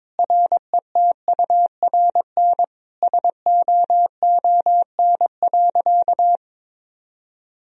Morse code, heard as a keyed tone, about 22 wpm, 700 Hz. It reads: RETURN SOON.
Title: Sentences from Top 1000 Words - Review Part 2 - 22wpm